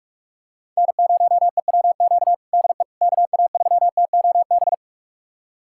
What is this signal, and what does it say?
Morse code, heard as a keyed tone, about 46 wpm, 700 Hz. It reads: N0EWQ DE KA3TOB
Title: USA Callsign DE Callsign 45wpm